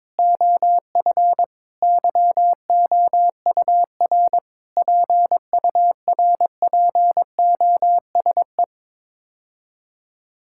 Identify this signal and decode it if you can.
Morse code, heard as a keyed tone, about 22 wpm, 700 Hz. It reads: OF YOUR PURPOSE
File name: Sets of 9 Words - 22wpm